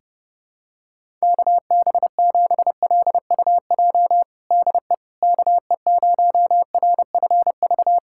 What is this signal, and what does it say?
Morse code, heard as a keyed tone, about 30 wpm, 700 Hz. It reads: KB7LUJ DE KE0RFV
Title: USA Callsign DE Callsign - 6 Characters - 30wpm